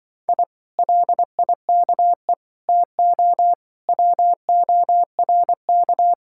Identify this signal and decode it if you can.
Morse code, heard as a keyed tone, about 23 wpm, 700 Hz. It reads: I LIKE TO WORK
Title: Speed-Racing Common QSO Phrases - 20wpm